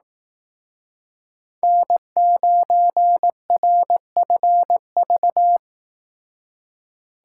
Morse code, heard as a keyed tone, about 18 wpm, 700 Hz. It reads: N9RFV
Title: Speed Racing - USA 1x3 Call Signs 15wpm